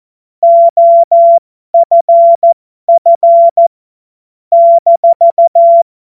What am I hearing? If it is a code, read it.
Morse code, heard as a keyed tone, about 14 wpm, spaced as for 12 wpm, 680 Hz. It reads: OFF -
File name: Morsekurs / Witze